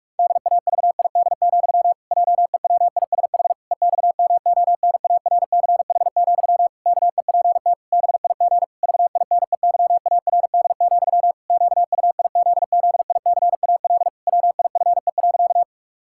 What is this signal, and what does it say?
Morse code, heard as a keyed tone, about 46 wpm, 700 Hz. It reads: DAVID, JEWISH EXMONARCH, KEPT BIG VINEYARD, QUIZZICAL WIFE.
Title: Pangram Challenge v2 - 45wpm